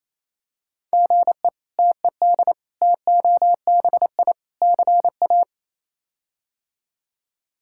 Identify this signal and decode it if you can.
Morse code, heard as a keyed tone, about 28 wpm, 700 Hz. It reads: GE TED TOBI CA
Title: K1USN SST QSO Practice - 28wpm